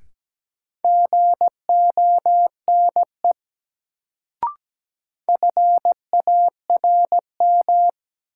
Morse code, heard as a keyed tone, about 17 wpm, 700 Hz. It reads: GONE FARM
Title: Sentences from Top 1200 Words - New Word Drill Alt - 17wpm